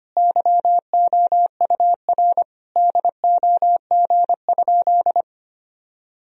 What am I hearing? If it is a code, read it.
Morse code, heard as a keyed tone, about 25 wpm, 700 Hz. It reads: YOUR DOG?
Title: Sentences from Top 1000 Words - Review Part 2 - 25wpm